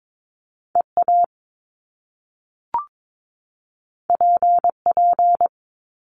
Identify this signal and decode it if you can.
Morse code, heard as a keyed tone, about 22 wpm, 700 Hz. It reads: EA PP
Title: Top Two Letter Combinations in Words 22wpm